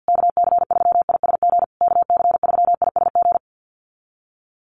Morse code, heard as a keyed tone, about 48 wpm, 700 Hz. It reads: KC3ISG KC3ISG
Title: USA Call Signs Repeated 3x 50wpm